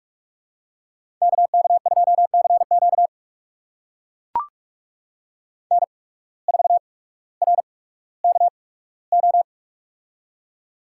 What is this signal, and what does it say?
Morse code, heard as a keyed tone, about 44 wpm, spaced as for 19 wpm, 700 Hz. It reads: KK2CQ N4RKO
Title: USA Call Signs - 5 Characters - 8x Character Spacing - 45wpm